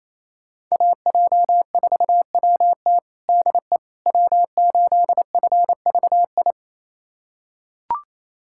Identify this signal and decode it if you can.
Morse code, heard as a keyed tone, about 28 wpm, 700 Hz. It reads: AJ4WT DE W8FVI
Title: USA Callsign DE Callsign - 5 Characters - 28wpm